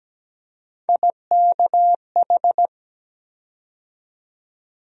Morse code, heard as a keyed tone, about 17 wpm, 700 Hz. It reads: IKH
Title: USA Call Sign Postfixes - 3 Letters 17wpm